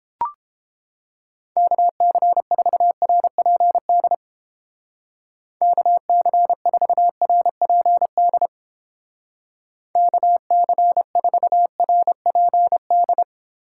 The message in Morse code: KC4RPD KC4RPD KC4RPD

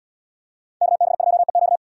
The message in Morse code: KB2EQ

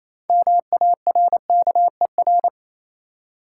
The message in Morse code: MARKER

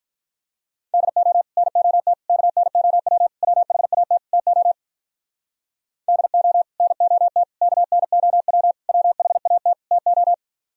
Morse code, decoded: DO NOT KNOW WHAT TO DO NOT KNOW WHAT TO

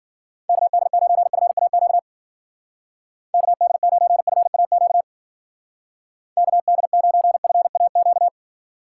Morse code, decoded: KD9FAQ KD9FAQ KD9FAQ

